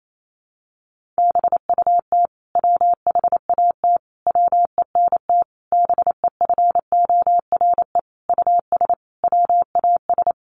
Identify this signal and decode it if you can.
Morse code, heard as a keyed tone, about 28 wpm, 700 Hz. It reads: BUT WHAT WENT BEFORE US WAS
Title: Sentences from Top 1400 Words 28wpm